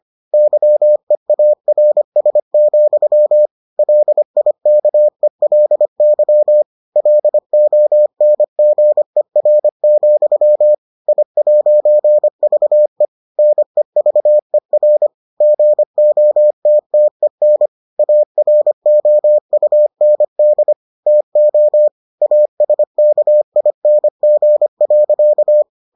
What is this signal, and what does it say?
Morse code, heard as a keyed tone, about 25 wpm, 600 Hz. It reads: YEARS, LIKELY LONGER, I'VE NEVER GOTTEN AROUND TO ASKING.